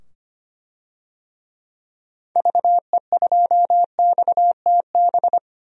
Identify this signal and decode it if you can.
Morse code, heard as a keyed tone, about 25 wpm, 700 Hz. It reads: VE2XTB
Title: DXCC-100 Call Signs - 6 Characters - 25wpm